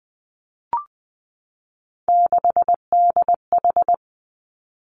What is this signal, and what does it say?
Morse code, heard as a keyed tone, about 20 wpm, 700 Hz. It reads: 6DH
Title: Three Challenging Letter-Number Combinations 20wpm